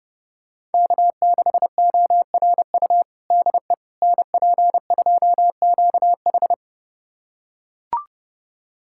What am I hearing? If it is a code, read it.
Morse code, heard as a keyed tone, about 32 wpm, 700 Hz. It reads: K6ORU DE NP2QH
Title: USA Callsign DE Callsign - 5 Characters - 30wpm